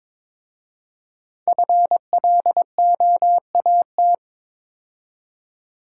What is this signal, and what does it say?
Morse code, heard as a keyed tone, about 22 wpm, 690 Hz. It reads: FLOAT
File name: 5 Letter Words - First 3 Letters Given - 22wpm